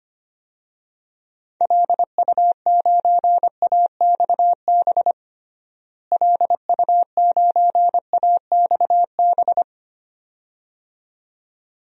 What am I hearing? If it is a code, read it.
Morse code, heard as a keyed tone, about 25 wpm, 700 Hz. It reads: LU9AXB LU9AXB
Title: DXCC-100 Call Signs - 6 Characters - Repeated 2x - 25wpm